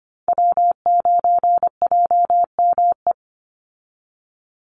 Morse code, heard as a keyed tone, about 25 wpm, 700 Hz. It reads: W9JME
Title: USA Call Signs 25wpm